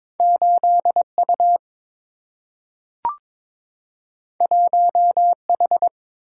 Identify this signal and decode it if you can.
Morse code, heard as a keyed tone, about 22 wpm, 700 Hz. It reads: 8U 1H
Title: Single-Letter Single-Number 22wpm